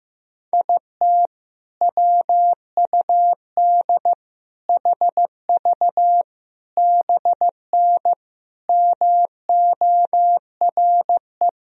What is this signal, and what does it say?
Morse code, heard as a keyed tone, about 15 wpm, 700 Hz. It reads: IT WUD HV BN MORE
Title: Sentences from Top 400 Words and Abbreviations 15wpm